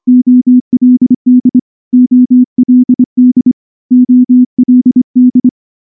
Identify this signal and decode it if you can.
Morse code, heard as a keyed tone, about 24 wpm, 250 Hz. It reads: OLD OLD OLD
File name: morse / morse-training-25WPM